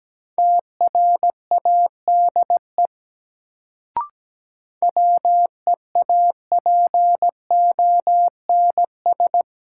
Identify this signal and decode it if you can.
Morse code, heard as a keyed tone, about 17 wpm, 700 Hz. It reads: TRADE WEAPONS
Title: Sets of 5 Words Encore - New Word Drill - 17wpm